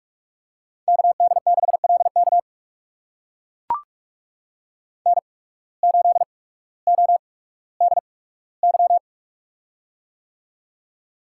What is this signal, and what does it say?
Morse code, heard as a keyed tone, about 44 wpm, spaced as for 19 wpm, 690 Hz. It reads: KD6LK N8ODY
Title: USA Call Signs - 5 Characters - 8x Character Spacing - 45wpm